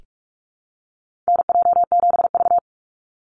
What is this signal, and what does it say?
Morse code, heard as a keyed tone, about 44 wpm, 700 Hz. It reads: NJ7V
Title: NA SOTA Activator Callsigns 45wpm